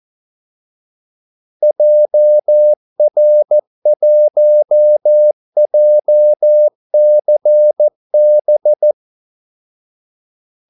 Morse code, heard as a keyed tone, about 14 wpm, 600 Hz. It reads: JR1JCB
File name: Callsigns-14wpm